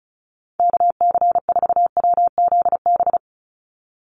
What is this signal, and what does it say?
Morse code, heard as a keyed tone, about 34 wpm, 700 Hz. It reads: KC4WZB
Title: NA SOTA Activator Callsigns 35wpm